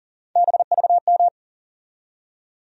Morse code, heard as a keyed tone, about 40 wpm, 700 Hz. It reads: BVM
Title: USA Call Sign Postfixes - 3 Letters 40wpm